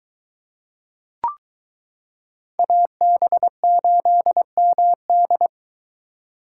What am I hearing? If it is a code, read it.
Morse code, heard as a keyed tone, about 22 wpm, 700 Hz. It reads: AB8MD